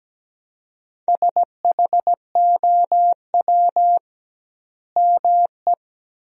Morse code, heard as a keyed tone, about 17 wpm, 700 Hz. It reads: SHOW ME
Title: 2x Word-Spacing - Sentences from Top 200 Words - 17wpm